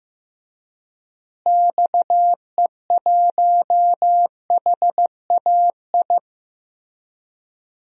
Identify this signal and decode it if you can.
Morse code, heard as a keyed tone, about 15 wpm, 700 Hz. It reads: XE1HAI
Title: DXCC-100 Call Signs - 6 Characters - Repeated 3x - 15wpm